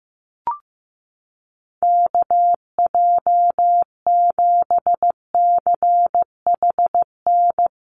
Morse code, heard as a keyed tone, about 15 wpm, 700 Hz. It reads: KJ7CHN
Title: USA Call Signs Repeated 3x 15wpm